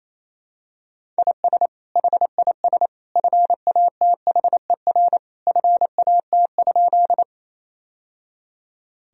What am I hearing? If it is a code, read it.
Morse code, heard as a keyed tone, about 28 wpm, 700 Hz. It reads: IS HIS FATHER FAT?
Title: Sentences from Top 1000 Words - Review Part 4 - 28wpm